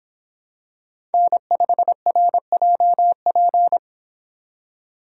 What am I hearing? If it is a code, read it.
Morse code, heard as a keyed tone, about 26 wpm, 700 Hz. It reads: N5RJP